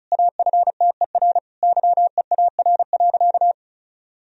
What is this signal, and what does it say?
Morse code, heard as a keyed tone, about 34 wpm, 690 Hz. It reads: AFTER YEAR.
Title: Sentences from Top 1000 Words - Review Part 1 - 35wpm